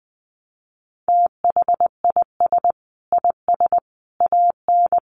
Morse code, heard as a keyed tone, about 20 wpm, 700 Hz. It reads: THIS IS AN
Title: Sentences from Top 1000 Words 20wpm